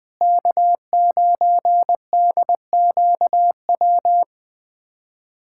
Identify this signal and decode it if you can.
Morse code, heard as a keyed tone, about 20 wpm, 700 Hz. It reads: K9DQW
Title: USA Callsign DE Callsign 20wpm